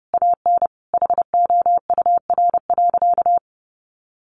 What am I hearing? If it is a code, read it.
Morse code, heard as a keyed tone, about 30 wpm, 700 Hz. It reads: AN HOUR.
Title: Sentences from Top 1100 Words 30wpm